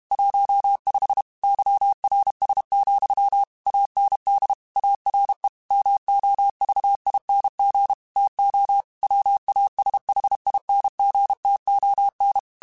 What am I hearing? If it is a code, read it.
Morse code, heard as a keyed tone, about 32 wpm, 800 Hz. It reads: 15 YRS, AND ARE MOVING TO WASHINGTON